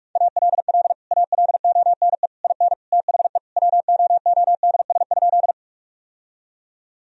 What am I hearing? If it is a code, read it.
Morse code, heard as a keyed tone, about 46 wpm, 700 Hz. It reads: ALL ALONE IN THE WOODS?
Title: Sentences from Top 1000 Words - Review Part 2 - 45wpm